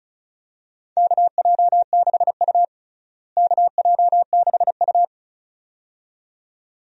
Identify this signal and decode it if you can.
Morse code, heard as a keyed tone, about 36 wpm, 700 Hz. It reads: KJ6U KJ6U
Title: USA Call Signs - 4 Characters - Repeated 2x - 35wpm